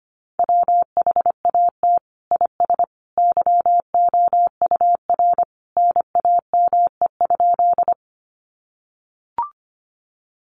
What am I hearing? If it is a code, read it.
Morse code, heard as a keyed tone, about 25 wpm, 700 Hz. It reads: WHAT IS YOUR NAME?
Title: Sentences from Top 200 Words 25wpm